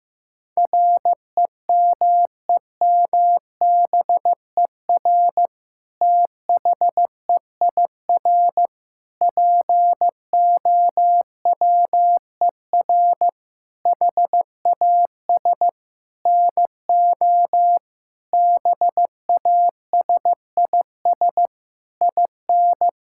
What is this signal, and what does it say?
Morse code, heard as a keyed tone, about 15 wpm, 700 Hz. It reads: REMEMBER THEIR POWER HAS NO BASIS IN
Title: Sentences from Top 1000 Words - Review Part 5 - 15wpm